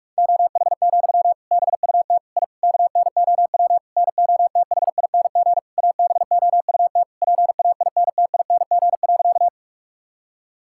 Morse code, heard as a keyed tone, about 46 wpm, 700 Hz. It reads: OH, BUT I KNOW NOTHING ABOUT PAINTING.
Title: Sentences from Top 1000 Words - Review Part 6 - 45wpm